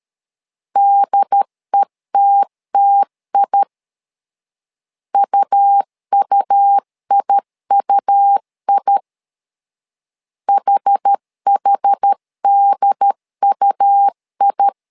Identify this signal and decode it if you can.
Morse code, heard as a keyed tone, about 13 wpm, 800 Hz. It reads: DETTI UUIUI HHDUI
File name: LEZIONI / 2 UHDN / 2 UHD 12wpm 1540